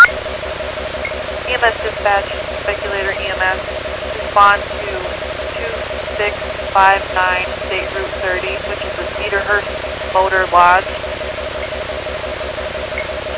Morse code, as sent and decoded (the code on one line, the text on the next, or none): none